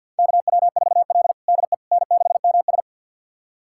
Morse code, heard as a keyed tone, about 53 wpm, 700 Hz. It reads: KW4L DE N6MS
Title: USA Callsign DE Callsign - 4 Characters - 50wpm